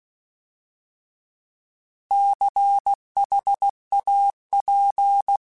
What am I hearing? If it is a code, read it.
Morse code, heard as a keyed tone, about 16 wpm, 800 Hz. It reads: CHAP
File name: Alphabetical / Alphabetical mp3s / Cs @15